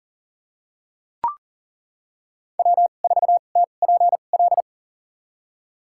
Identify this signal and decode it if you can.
Morse code, heard as a keyed tone, about 40 wpm, spaced as for 29 wpm, 700 Hz. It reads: W4TPL
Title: USA Call Signs - 5 Characters - 2x Character Spacing - 40wpm